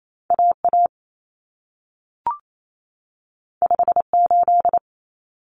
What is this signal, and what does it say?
Morse code, heard as a keyed tone, about 28 wpm, 700 Hz. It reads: AA 58
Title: Common Cut Numbers - 2-Digit 28wpm